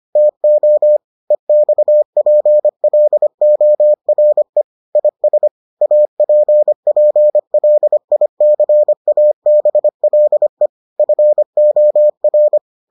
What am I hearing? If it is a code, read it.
Morse code, heard as a keyed tone, about 25 wpm, 600 Hz. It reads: TO EXPLORE IS APPLICABLE FOR